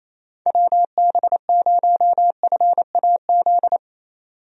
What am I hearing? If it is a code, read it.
Morse code, heard as a keyed tone, about 28 wpm, 700 Hz. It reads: WB0FAZ
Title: USA Call Signs - 6 Characters - 28wpm